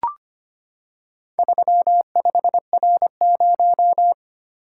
Morse code, heard as a keyed tone, about 25 wpm, 700 Hz. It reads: 35R0
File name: practice / 25wpm / Four Letter-Number Combinations - 25wpm